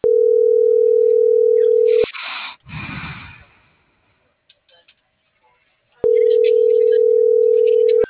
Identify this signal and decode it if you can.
Ringback tone (North America), 440 + 480 Hz.